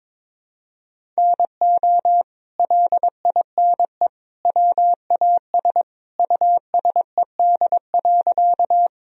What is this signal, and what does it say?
Morse code, heard as a keyed tone, about 21 wpm, 700 Hz. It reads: NO LINE WAS USED.
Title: Sentences from Top 1000 Words - Review Part 1 - 22wpm